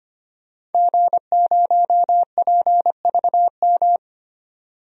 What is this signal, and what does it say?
Morse code, heard as a keyed tone, about 25 wpm, 700 Hz. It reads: G0PVM
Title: DXCC-100 Call Signs - 5 Characters - 25wpm